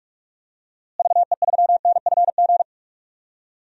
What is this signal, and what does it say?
Morse code, heard as a keyed tone, about 41 wpm, 700 Hz. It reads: VE3NFG
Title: DXCC-100 Call Signs - 6 Characters - Repeated 3x - 45wpm